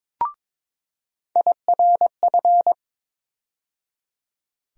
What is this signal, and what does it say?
Morse code, heard as a keyed tone, about 21 wpm, 700 Hz. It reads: IRF
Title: USA Call Sign Postfixes - 3 Letters 22wpm